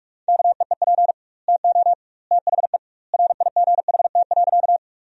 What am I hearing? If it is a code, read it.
Morse code, heard as a keyed tone, about 42 wpm, 700 Hz. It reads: KEEP TO THE RIGHT.
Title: 2x Word-Spacing - Sentences from Top 300 Words - 45wpm